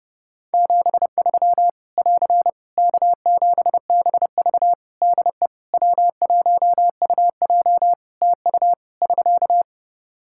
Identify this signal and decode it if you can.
Morse code, heard as a keyed tone, about 30 wpm, 700 Hz. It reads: 73 <AR> K7BV DE W1UJ TU <SK>
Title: K1USN SST QSO Practice 30wpm